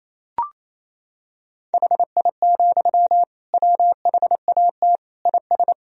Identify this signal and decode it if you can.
Morse code, heard as a keyed tone, about 28 wpm, 700 Hz. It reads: HI, WHAT IS
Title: Sentences from Top 1000 Words - Review Part 3 - 28wpm